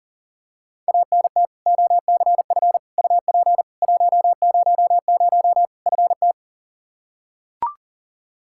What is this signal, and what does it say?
Morse code, heard as a keyed tone, about 37 wpm, 700 Hz. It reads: ANT OCF UP 100 FT
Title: Practice QSO - Exchange 2 - Info - 40wpm